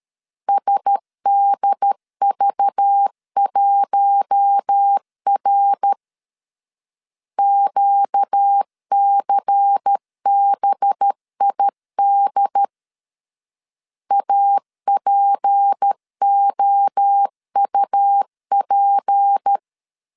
Morse code, heard as a keyed tone, about 13 wpm, 800 Hz. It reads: SDV1R QCBID APOUP
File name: LEZIONI / 6 PKX0 / 6 P 12wpm 2033